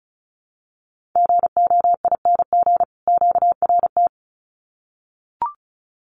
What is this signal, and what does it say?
Morse code, heard as a keyed tone, about 34 wpm, 700 Hz. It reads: GOING QRT
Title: Common QSO Phrases 35wpm